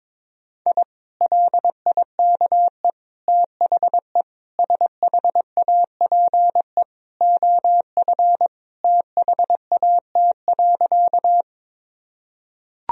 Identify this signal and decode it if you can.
Morse code, heard as a keyed tone, about 22 wpm, 700 Hz. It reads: I LIKE THE SHAPE OF THAT.
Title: Sentences from Top 1000 Words - Review Part 3 - 22wpm